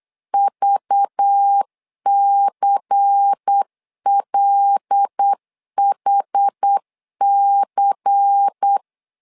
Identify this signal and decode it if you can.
Morse code, heard as a keyed tone, about 8 wpm, 800 Hz. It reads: VCLHC
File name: LEZIONI / 5 LBVC / 5 LBVC 8wpm 2935